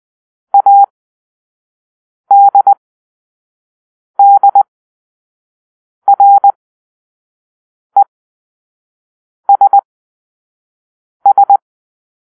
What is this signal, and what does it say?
Morse code, heard as a keyed tone, about 20 wpm, spaced as for 5 wpm, 800 Hz. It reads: ADDRESS